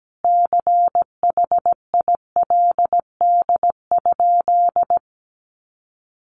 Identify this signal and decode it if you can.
Morse code, heard as a keyed tone, about 17 wpm, 700 Hz. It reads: CHILD?
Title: Sentences from Top 1000 Words 17wpm